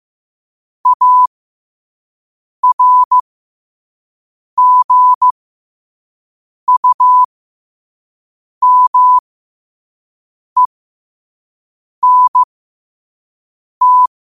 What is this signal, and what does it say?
Morse code, heard as a keyed tone, about 15 wpm, spaced as for 5 wpm, 1000 Hz. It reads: ARGUMENT
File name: qotdfiles / 5wpm